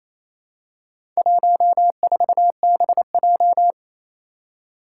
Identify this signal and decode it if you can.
Morse code, heard as a keyed tone, about 29 wpm, 700 Hz. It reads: 14BJ
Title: Four Challenging Letter-Number Combinations 28wpm